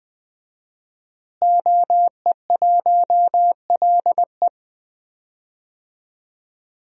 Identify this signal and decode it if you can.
Morse code, heard as a keyed tone, about 21 wpm, 700 Hz. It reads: OE1LE
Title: DXCC-100 Call Signs - 5 Characters - Repeated 3x - 20wpm